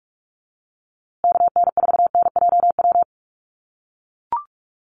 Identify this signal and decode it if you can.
Morse code, heard as a keyed tone, about 44 wpm, 700 Hz. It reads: KN4NJW
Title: USA Call Signs Repeated 2x 45wpm